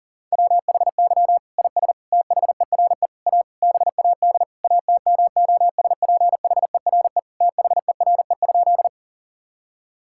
Morse code, heard as a keyed tone, about 40 wpm, 700 Hz. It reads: WHY IS THERE A BAD ATMOSPHERE THERE?
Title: Sentences from Top 1600 Words 40wpm